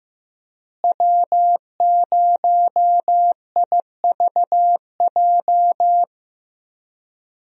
Morse code, heard as a keyed tone, about 15 wpm, 700 Hz. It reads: W0IVJ
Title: USA Call Signs - 5 Characters - Repeated 2x - 15wpm